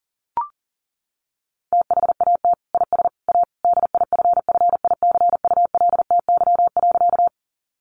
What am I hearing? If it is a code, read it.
Morse code, heard as a keyed tone, about 40 wpm, 700 Hz. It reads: THAT IS A DIFFICULTY.
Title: Sentences from Top 1500 Words 40wpm